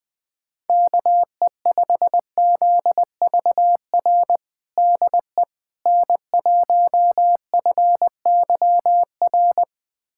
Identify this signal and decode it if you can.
Morse code, heard as a keyed tone, about 21 wpm, 700 Hz. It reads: KE5ZVR DE N1FYR
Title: USA Callsign DE Callsign 20wpm